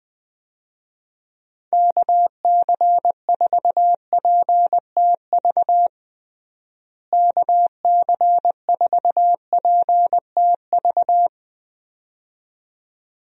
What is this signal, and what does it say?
Morse code, heard as a keyed tone, about 20 wpm, 700 Hz. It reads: KC4PTV KC4PTV
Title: USA Call Signs - 6 Characters - Repeated 2x - 20wpm